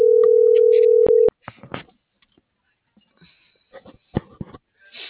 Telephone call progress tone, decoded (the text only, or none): ringback tone